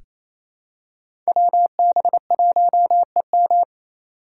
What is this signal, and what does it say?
Morse code, heard as a keyed tone, about 29 wpm, 700 Hz. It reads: WB1EM
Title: Code Groups - 5 Letter:Numbers - 28wpm